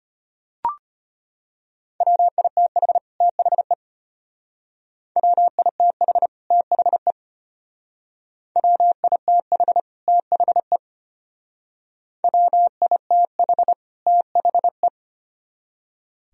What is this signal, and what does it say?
Morse code, heard as a keyed tone, about 32 wpm, 700 Hz. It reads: WITH THE WITH THE WITH THE WITH THE